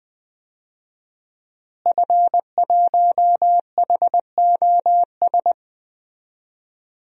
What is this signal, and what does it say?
Morse code, heard as a keyed tone, about 21 wpm, 700 Hz. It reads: F1HOS
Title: DXCC-100 Call Signs - 5 Characters - Repeated 2x - 20wpm